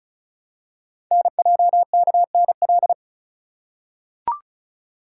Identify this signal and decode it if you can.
Morse code, heard as a keyed tone, about 36 wpm, 690 Hz. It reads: NJKNL
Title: Code Groups - 5 Letter:Numbers - 35wpm